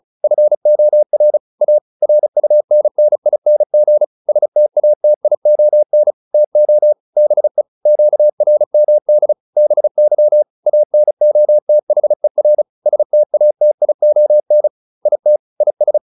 Morse code, heard as a keyed tone, about 34 wpm, 600 Hz. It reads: FOR A RUNNING STATION TO BE QRMD BY ANOTHER STATION IT IS
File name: SS309 35